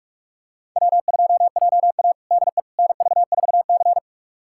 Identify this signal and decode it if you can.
Morse code, heard as a keyed tone, about 46 wpm, 700 Hz. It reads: W2JA DE NV4C
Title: USA Callsign DE Callsign - 4 Characters - 45wpm